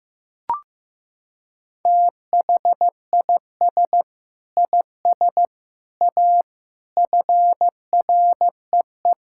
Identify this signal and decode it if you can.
Morse code, heard as a keyed tone, about 15 wpm, 700 Hz. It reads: THIS IS A FREE
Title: Sentences from Top 1000 Words - Review Part 6 - 15wpm